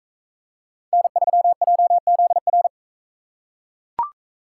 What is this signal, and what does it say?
Morse code, heard as a keyed tone, about 40 wpm, 700 Hz. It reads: N3JZR